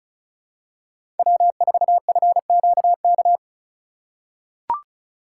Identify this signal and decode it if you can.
Morse code, heard as a keyed tone, about 34 wpm, 700 Hz. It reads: W4FQK
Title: USA Call Signs - 5 Characters - 4x Character Spacing - 35wpm